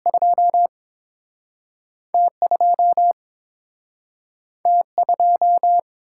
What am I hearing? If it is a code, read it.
Morse code, heard as a keyed tone, about 30 wpm, 700 Hz. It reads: 2 T2 T2